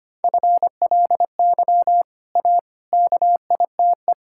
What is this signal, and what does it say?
Morse code, heard as a keyed tone, about 25 wpm, 700 Hz. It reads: FLY A KITE
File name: Easy Sentences with Hints - 25wpm